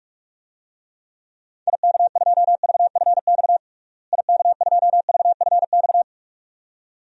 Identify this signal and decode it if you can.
Morse code, heard as a keyed tone, about 46 wpm, 700 Hz. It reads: IK2VFX IK2VFX
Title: DXCC-100 Call Signs - 6 Characters - Repeated 2x - 45wpm